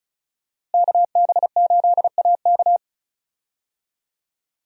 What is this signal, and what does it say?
Morse code, heard as a keyed tone, about 36 wpm, 700 Hz. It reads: KB8AK
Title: USA Call Signs - 5 Characters - Repeated 3x - 35wpm